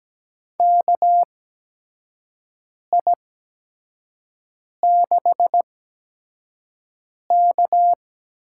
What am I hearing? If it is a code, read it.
Morse code, heard as a keyed tone, about 17 wpm, spaced as for 5 wpm, 700 Hz. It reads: KI6K
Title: USA Call Signs - 6 Characters - 8x Character Spacing - 17wpm